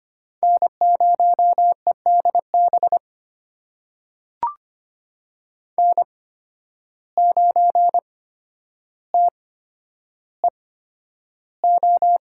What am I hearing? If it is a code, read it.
Morse code, heard as a keyed tone, about 25 wpm, spaced as for 11 wpm, 700 Hz. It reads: N0EDB N9TEO
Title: USA Call Signs - 5 Characters - 8x Character Spacing - 25wpm